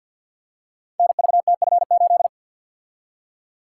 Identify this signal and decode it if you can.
Morse code, heard as a keyed tone, about 46 wpm, 700 Hz. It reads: NVTF8
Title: Code Groups - 5 Letter:Numbers - 50wpm